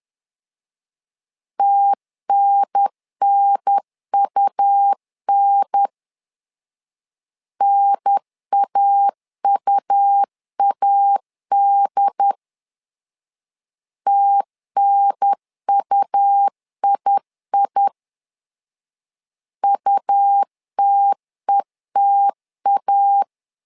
Morse code, heard as a keyed tone, about 11 wpm, 800 Hz. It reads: TNNUN NAUAD TNUII UTETA
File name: LEZIONI / 3 MWRO / 3 M 10wpm 1828